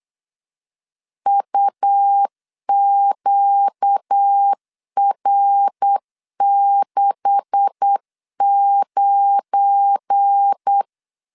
Morse code, heard as a keyed tone, about 8 wpm, 800 Hz. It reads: UQR69
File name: 8wpm